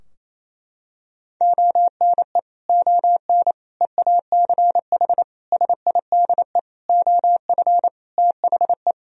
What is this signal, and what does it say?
Morse code, heard as a keyed tone, about 28 wpm, 700 Hz. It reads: ONE ON EACH SIDE OF THE
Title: Sentences from Top 1000 Words - Review Part 2 - 28wpm